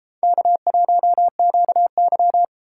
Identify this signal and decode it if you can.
Morse code, heard as a keyed tone, about 33 wpm, 700 Hz. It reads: K1QY